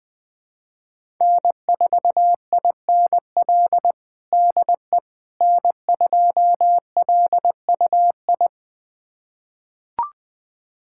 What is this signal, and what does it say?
Morse code, heard as a keyed tone, about 20 wpm, 700 Hz. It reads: N4INL DE N2LUI
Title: USA Callsign DE Callsign - 5 Characters - 20wpm